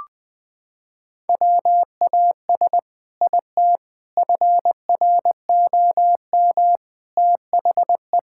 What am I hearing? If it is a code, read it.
Morse code, heard as a keyed tone, about 20 wpm, 700 Hz. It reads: WAS IT FROM THE
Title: Sentences from Top 1600 Words 20wpm